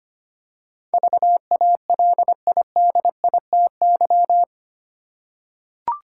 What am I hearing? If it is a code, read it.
Morse code, heard as a keyed tone, about 25 wpm, 700 Hz. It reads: VALIDITY
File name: Eight Letter Words - 25wpm